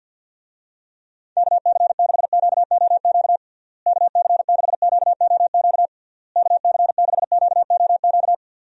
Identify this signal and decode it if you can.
Morse code, heard as a keyed tone, about 50 wpm, 700 Hz. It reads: KC6QOQ KC6QOQ KC6QOQ
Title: USA Call Signs - 6 Characters - Repeated 3x - 50wpm